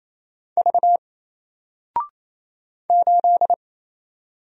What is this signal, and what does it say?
Morse code, heard as a keyed tone, about 27 wpm, 700 Hz. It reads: V 8
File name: Single-Letter-Number-28wpm